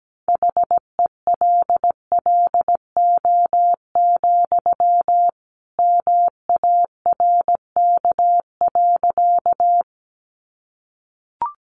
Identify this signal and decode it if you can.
Morse code, heard as a keyed tone, about 17 wpm, 700 Hz. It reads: HELLO, MARK.
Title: Sentences from Top 1100 Words 17wpm